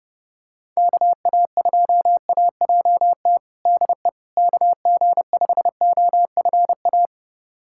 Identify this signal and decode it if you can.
Morse code, heard as a keyed tone, about 31 wpm, 700 Hz. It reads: KA2AJT DE KG5OFA
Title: USA Callsign DE Callsign - 6 Characters - 30wpm